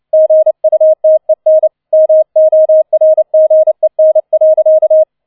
Morse code, heard as a keyed tone, about 29 wpm, 630 Hz. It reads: GUTEN MORGEN.